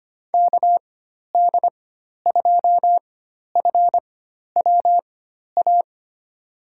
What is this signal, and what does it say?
Morse code, heard as a keyed tone, about 25 wpm, spaced as for 12 wpm, 700 Hz. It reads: KD2FWA
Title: USA Call Signs - 4x Character Spacing - 25wpm